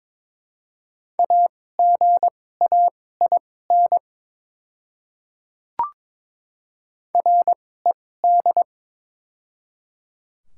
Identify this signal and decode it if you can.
Morse code, heard as a keyed tone, about 22 wpm, spaced as for 16 wpm, 700 Hz. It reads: AGAIN RED
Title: 2x Character Spacing - Top 500 Words - 22wpm